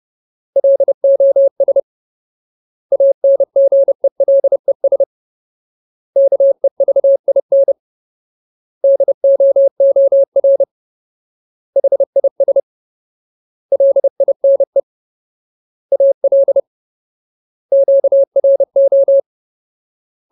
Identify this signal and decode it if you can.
Morse code, heard as a keyed tone, about 30 wpm, 550 Hz. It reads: LOS ANGELES KEVIN DOOR HIS LINE AL QRO
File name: Ultimate Word List / 30WPM Ultimate